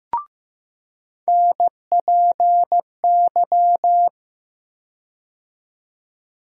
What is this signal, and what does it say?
Morse code, heard as a keyed tone, about 15 wpm, 700 Hz. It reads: NPY
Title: USA Call Sign Postfixes - 3 Letters 15wpm